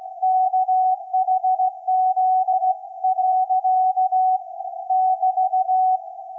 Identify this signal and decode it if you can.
Morse code, heard as a keyed tone, about 17 wpm, 750 Hz. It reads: KHZ. <BT>
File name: iz7slz / VLF / SAQ 24-12-2024